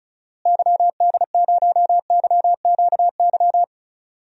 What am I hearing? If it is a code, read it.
Morse code, heard as a keyed tone, about 34 wpm, 700 Hz. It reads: YD0YQY
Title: DXCC-100 Call Signs - 6 Characters - Repeated 2x - 35wpm